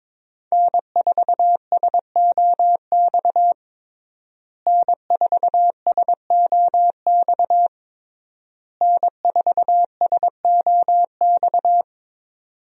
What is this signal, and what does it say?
Morse code, heard as a keyed tone, about 22 wpm, 700 Hz. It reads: N4SOX N4SOX N4SOX
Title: USA Call Signs - 5 Characters - Repeated 3x - 22wpm